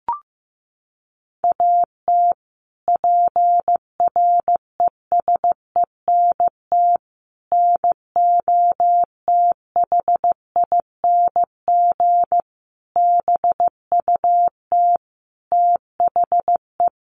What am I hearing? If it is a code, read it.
Morse code, heard as a keyed tone, about 15 wpm, 700 Hz. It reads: AT PRESENT NOTHING BUT THE
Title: Sentences from Top 1500 Words 15wpm